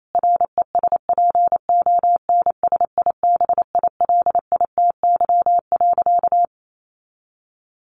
Morse code, heard as a keyed tone, about 28 wpm, 700 Hz. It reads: RESPONSIBILITY.
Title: Sentences from Top 1500 Words 28wpm